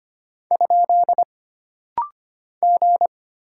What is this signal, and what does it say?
Morse code, heard as a keyed tone, about 25 wpm, 700 Hz. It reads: ? G
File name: Single Character (Letter:Number:Punctuation:Prosign) - Rapid-Fire - 25wpm